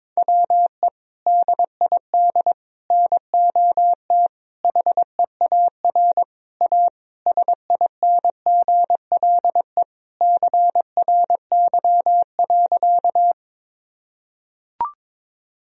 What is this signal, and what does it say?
Morse code, heard as a keyed tone, about 21 wpm, 700 Hz. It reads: WE DID NOT HEAR A SINGLE CRY.
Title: Sentences from Top 1000 Words - Review Part 4 - 22wpm